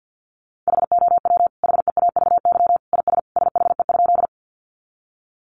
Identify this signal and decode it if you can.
Morse code, heard as a keyed tone, about 49 wpm, 700 Hz. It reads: HOW HEAVY IS SHE?
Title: Sentences from Top 1100 Words 50wpm